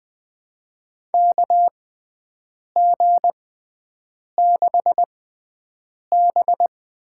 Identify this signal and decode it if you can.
Morse code, heard as a keyed tone, about 21 wpm, spaced as for 7 wpm, 700 Hz. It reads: KG6B
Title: USA Call Signs - 6 Characters - 6x Character Spacing - 20wpm